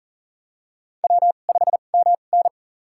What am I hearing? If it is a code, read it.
Morse code, heard as a keyed tone, about 40 wpm, spaced as for 29 wpm, 700 Hz. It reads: W5MN
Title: USA Call Signs - 2x Character Spacing - 40wpm